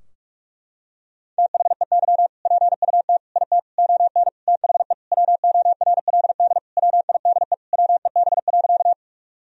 Morse code, heard as a keyed tone, about 46 wpm, 700 Hz. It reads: THEY PUT IT ON THE WORLD WIDE WEB.
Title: Sentences from Top 1000 Words - Review Part 4 - 45wpm